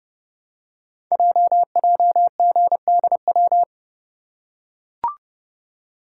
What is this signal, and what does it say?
Morse code, heard as a keyed tone, about 30 wpm, 700 Hz. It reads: JJGDW
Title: Code Groups - 5 Letters - 30wpm